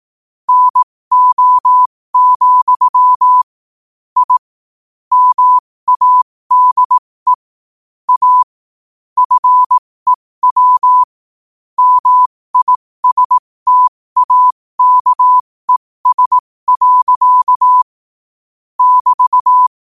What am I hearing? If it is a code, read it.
Morse code, heard as a keyed tone, about 18 wpm, spaced as for 15 wpm, 1000 Hz. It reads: NO, I MADE A FEW MISTAKES. <BT>
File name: qotdfiles / 15wpm